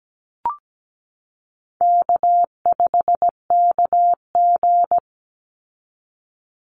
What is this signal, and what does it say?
Morse code, heard as a keyed tone, about 17 wpm, 700 Hz. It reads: K5KG
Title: Easy Call Signs Repeated 3x - 17wpm